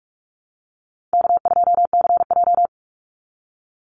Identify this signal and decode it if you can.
Morse code, heard as a keyed tone, about 44 wpm, 700 Hz. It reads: K2CJ
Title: Easy Call Signs - 45wpm